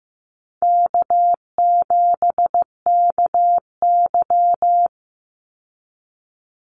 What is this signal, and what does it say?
Morse code, heard as a keyed tone, about 15 wpm, 700 Hz. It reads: K7KY
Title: Call Signs Repeated 3x - 15wpm